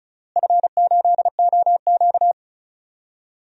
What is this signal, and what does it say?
Morse code, heard as a keyed tone, about 36 wpm, 700 Hz. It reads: F8OQ
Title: DXCC-100 Call Signs - 4 Characters - Repeated 3x - 35wpm